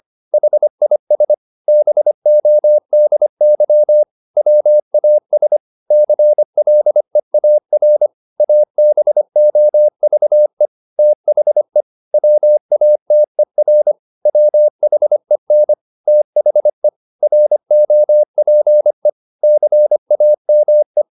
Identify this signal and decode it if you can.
Morse code, heard as a keyed tone, about 25 wpm, 600 Hz. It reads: HIS BODY WAS CLEAR ABOVE THE WATER WHEN THE ROPE CAME